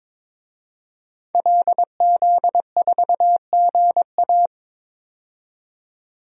Morse code, heard as a keyed tone, about 22 wpm, 700 Hz. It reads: LZ4GA
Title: DXCC-100 Call Signs - 5 Characters - 22wpm